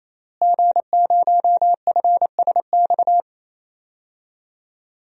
Morse code, heard as a keyed tone, about 28 wpm, 700 Hz. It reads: G0FSX
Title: DXCC-100 Call Signs - 5 Characters - Repeated 3x - 28wpm